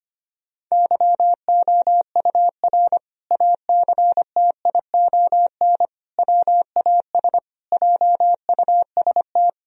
Morse code, heard as a keyed tone, about 25 wpm, 700 Hz. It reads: YOUR ACTION WAS JUST
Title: Sentences from Top 1000 Words - Review Part 4 - 25wpm